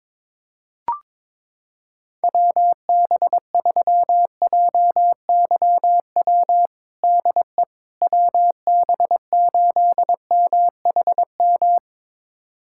Morse code, heard as a keyed tone, about 22 wpm, 700 Hz. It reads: WB3JYW DE WB8MHM
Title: USA Callsign DE Callsign - 6 Characters - 22wpm